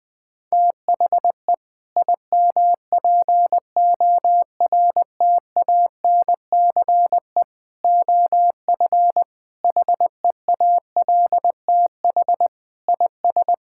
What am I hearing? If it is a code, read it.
Morse code, heard as a keyed tone, about 21 wpm, 700 Hz. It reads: THE IMPORTANCE OF HEALTH IS
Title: Sentences with 3-Word Phrases - Spoken Beginning - 20wpm